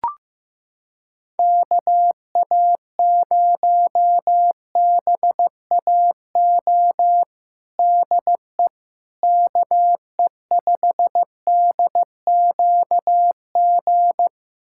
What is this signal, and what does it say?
Morse code, heard as a keyed tone, about 15 wpm, 700 Hz. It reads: KA0BAO DE KE5DQG
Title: USA Callsign DE Callsign 15wpm